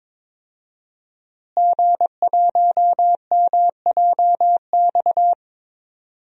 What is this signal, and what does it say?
Morse code, heard as a keyed tone, about 22 wpm, 700 Hz. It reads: G1MJX
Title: DXCC-100 Call Signs - 5 Characters - Repeated 3x - 22wpm